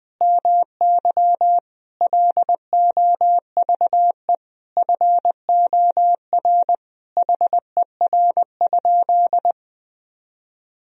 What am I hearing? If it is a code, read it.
Morse code, heard as a keyed tone, about 21 wpm, 700 Hz. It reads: MY LOVE FOR HER?
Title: Sentences from Top 1000 Words - Review Part 1 - 20wpm